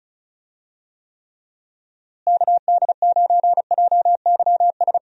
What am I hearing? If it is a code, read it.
Morse code, heard as a keyed tone, about 34 wpm, 700 Hz. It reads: KD9JYS
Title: USA Call Signs - 6 Characters - Repeated 3x - 35wpm